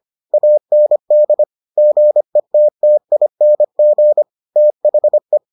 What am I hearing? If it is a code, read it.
Morse code, heard as a keyed tone, about 25 wpm, 600 Hz. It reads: AND GETTING THE